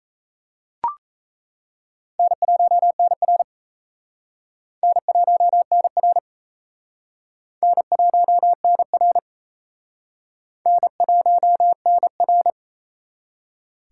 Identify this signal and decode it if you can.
Morse code, heard as a keyed tone, about 36 wpm, 700 Hz. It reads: N1NR N1NR N1NR N1NR